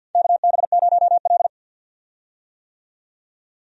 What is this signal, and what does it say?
Morse code, heard as a keyed tone, about 50 wpm, 700 Hz. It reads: KB0L
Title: USA Call Signs - 4 Characters - 50wpm